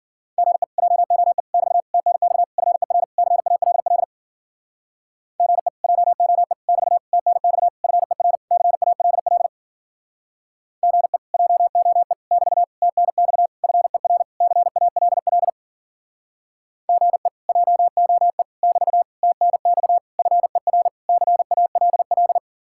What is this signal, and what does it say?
Morse code, heard as a keyed tone, about 55 wpm, 700 Hz. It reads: GE JOE <BT> TNX FER CALL GE JOE <BT> TNX FER CALL GE JOE <BT> TNX FER CALL GE JOE <BT> TNX FER CALL